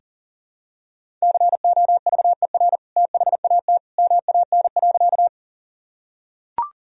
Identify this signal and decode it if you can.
Morse code, heard as a keyed tone, about 37 wpm, 700 Hz. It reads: COVER THAT MAN.
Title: Sentences from Top 1000 Words - Review Part 1 - 40wpm